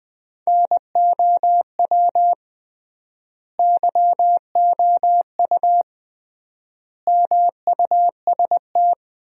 Morse code, heard as a keyed tone, about 20 wpm, 700 Hz. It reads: NOW YOU MUST
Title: 3x Word-Spacing - Sentences from Top 300 Words - 20wpm